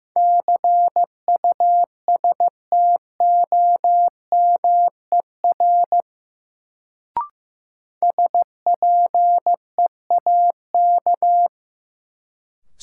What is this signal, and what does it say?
Morse code, heard as a keyed tone, about 15 wpm, 700 Hz. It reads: CUSTOMER SPEAK
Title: Sentences from Top 1100 Words - New Word Drill Alt - 15wpm